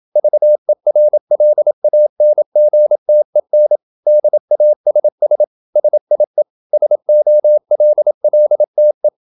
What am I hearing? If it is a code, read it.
Morse code, heard as a keyed tone, about 27 wpm, 600 Hz. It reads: VERLANGTEN DASS SIE SOLLTE